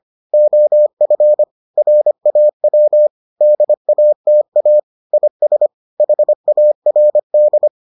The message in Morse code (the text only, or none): OF RAW DATA IS HARD